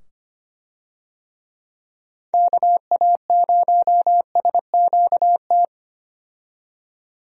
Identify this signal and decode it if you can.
Morse code, heard as a keyed tone, about 25 wpm, 700 Hz. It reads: KA0SQT
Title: USA Call Signs - 6 Characters - 25wpm